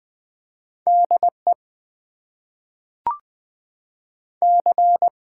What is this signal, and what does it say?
Morse code, heard as a keyed tone, about 20 wpm, 700 Hz. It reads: DE C
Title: Top 90 QSO Elements - 8x Character Spacing - 20wpm